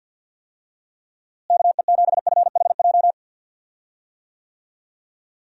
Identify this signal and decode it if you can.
Morse code, heard as a keyed tone, about 47 wpm, 700 Hz. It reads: KE7FHJ
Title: USA Call Signs - 6 Characters - Repeated 2x - 50wpm